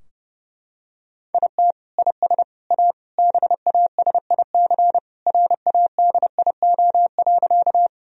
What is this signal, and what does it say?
Morse code, heard as a keyed tone, about 32 wpm, 700 Hz. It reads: IT IS A BASIC RADIO.
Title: Sentences from Top 1000 Words - Review Part 3 - 30wpm